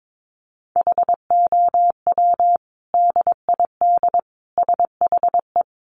Morse code, heard as a keyed tone, about 22 wpm, 700 Hz. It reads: HOW DID SHE
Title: Sentences from Top 800 Words 22wpm